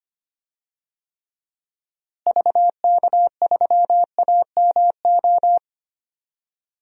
Morse code, heard as a keyed tone, about 25 wpm, 700 Hz. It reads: VK3AMO
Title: DXCC-100 Call Signs - 6 Characters - Repeated 3x - 25wpm